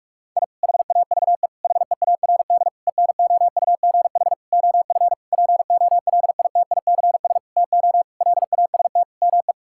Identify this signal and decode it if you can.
Morse code, heard as a keyed tone, about 46 wpm, 700 Hz. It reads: I HAVE HEARD ENOUGH OF POLITICS TO LAST ME
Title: Sentences from Top 1000 Words - Review Part 6 - 45wpm